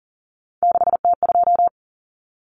Morse code, heard as a keyed tone, about 40 wpm, 700 Hz. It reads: 6T2
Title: Common Cut Numbers - 3-Digit 40wpm